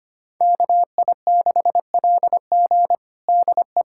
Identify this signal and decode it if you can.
Morse code, heard as a keyed tone, about 25 wpm, 700 Hz. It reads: KI6LG DE
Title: USA Callsign DE Callsign - 5 Characters - 25wpm